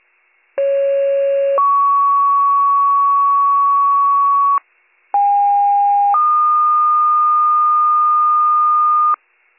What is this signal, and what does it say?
Two-tone page, heard as a one-second tone, then a three-second tone, about 550 then 1120 Hz, about 810 then 1220 Hz. It is heard from 0.6 to 9.1 s.